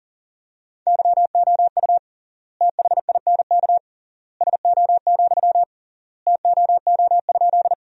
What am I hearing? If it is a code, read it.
Morse code, heard as a keyed tone, about 40 wpm, 700 Hz. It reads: YOU THINK SO, TOO?
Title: 3x Word-Spacing - Sentences from Top 200 Words - 40wpm